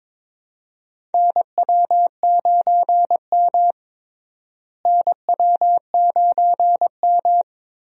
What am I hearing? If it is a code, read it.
Morse code, heard as a keyed tone, about 22 wpm, 700 Hz. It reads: NW9M NW9M
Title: USA Call Signs - 4 Characters - Repeated 2x - 22wpm